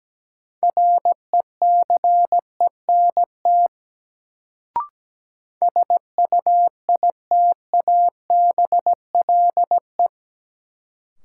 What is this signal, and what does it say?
Morse code, heard as a keyed tone, about 17 wpm, 700 Hz. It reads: RECENT SUITABLE
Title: Sentences from Top 1400 Words - New Word Drill Alt - 17wpm